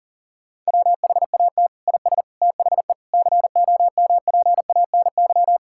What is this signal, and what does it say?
Morse code, heard as a keyed tone, about 40 wpm, 700 Hz. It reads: WHAT IS THE COMPANY
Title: Sentences from Top 1000 Words - Review Part 5 - 40wpm